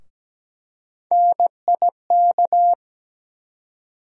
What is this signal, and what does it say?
Morse code, heard as a keyed tone, about 17 wpm, 700 Hz. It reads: NIK
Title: USA Call Sign Postfixes - 3 Letters 17wpm